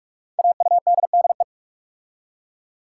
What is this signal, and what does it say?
Morse code, heard as a keyed tone, about 44 wpm, 700 Hz. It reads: AUDDE
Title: Code Groups - 5 Letters - 45wpm